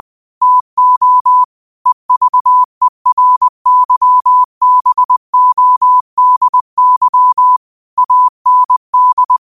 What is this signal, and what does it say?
Morse code, heard as a keyed tone, about 20 wpm, 1000 Hz. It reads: TO EVERYBODY AND